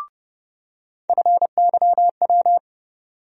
Morse code, heard as a keyed tone, about 30 wpm, 700 Hz. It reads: FYW
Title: USA Call Sign Postfixes - 3 Letters 30wpm